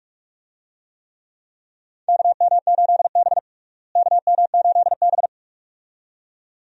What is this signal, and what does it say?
Morse code, heard as a keyed tone, about 46 wpm, 700 Hz. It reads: KM8B KM8B
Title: USA Call Signs - 4 Characters - Repeated 2x - 45wpm